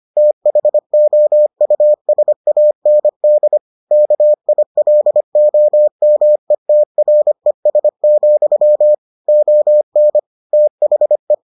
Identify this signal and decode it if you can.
Morse code, heard as a keyed tone, about 25 wpm, 600 Hz. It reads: THOUSAND KILOMETRES, ON THE